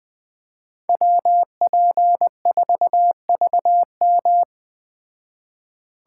Morse code, heard as a keyed tone, about 20 wpm, 700 Hz. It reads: WP4VM